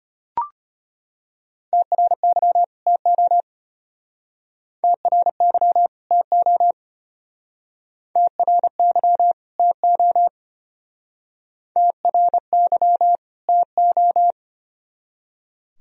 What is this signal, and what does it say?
Morse code, heard as a keyed tone, about 33 wpm, 700 Hz. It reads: TRY TO TRY TO TRY TO TRY TO